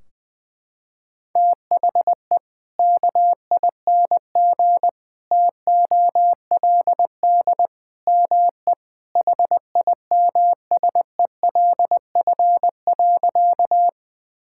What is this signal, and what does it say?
Morse code, heard as a keyed tone, about 20 wpm, 700 Hz. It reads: THE KING TOLD ME HIMSELF.
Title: Sentences from Top 1600 Words 20wpm